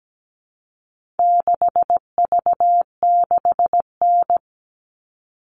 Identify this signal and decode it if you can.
Morse code, heard as a keyed tone, about 17 wpm, 700 Hz. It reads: 6V6N
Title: All Cut Numbers - 4-Digit 17wpm